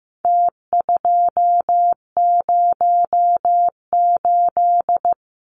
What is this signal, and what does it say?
Morse code, heard as a keyed tone, about 15 wpm, 700 Hz. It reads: T208
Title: Numbers - Contest Style - 4-Digit 15wpm